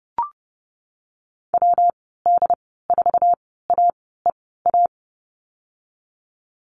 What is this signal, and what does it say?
Morse code, heard as a keyed tone, about 30 wpm, spaced as for 17 wpm, 700 Hz. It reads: WD4AEA
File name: Lesson 19g - Letter w - Call Signs 30wpm